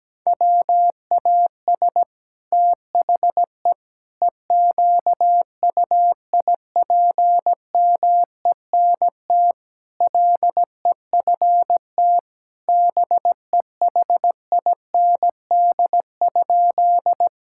Morse code, heard as a keyed tone, about 17 wpm, 700 Hz. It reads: WAS THE EQUIPMENT LEFT BEHIND?
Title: Sentences from Top 1000 Words - Review Part 5 - 17wpm